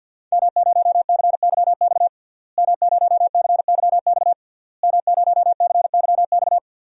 Morse code, heard as a keyed tone, about 51 wpm, 700 Hz. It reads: M0CYX M0CYX M0CYX
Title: DXCC-100 Call Signs - 5 Characters - Repeated 3x - 50wpm